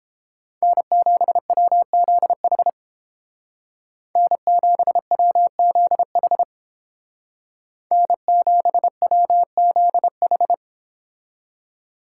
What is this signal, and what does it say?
Morse code, heard as a keyed tone, about 30 wpm, 700 Hz. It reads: N7WZH N7WZH N7WZH